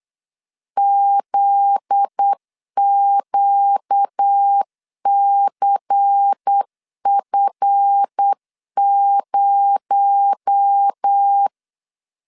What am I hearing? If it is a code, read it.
Morse code, heard as a keyed tone, about 8 wpm, 800 Hz. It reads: ZQCF0
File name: LEZIONI / 7 GF9Z / 7 GF9Z 8wpm 3213